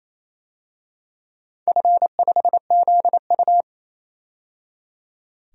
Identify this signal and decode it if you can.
Morse code, heard as a keyed tone, about 28 wpm, 700 Hz. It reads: F5ZU
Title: DXCC-100 Call Signs - 4 Characters - 28wpm